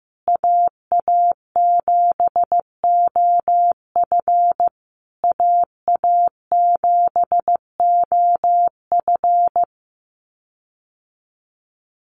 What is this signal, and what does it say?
Morse code, heard as a keyed tone, about 15 wpm, 700 Hz. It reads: AA7OF AA7OF
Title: USA Call Signs Repeated 3x 15wpm